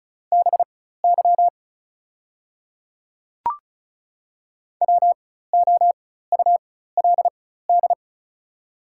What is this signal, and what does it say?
Morse code, heard as a keyed tone, about 35 wpm, spaced as for 16 wpm, 690 Hz. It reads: BY WOULD
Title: 4x Character Spacing - Top 100 Words - 35wpm